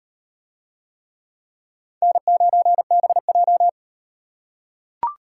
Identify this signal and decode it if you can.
Morse code, heard as a keyed tone, about 38 wpm, 700 Hz. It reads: N9BJ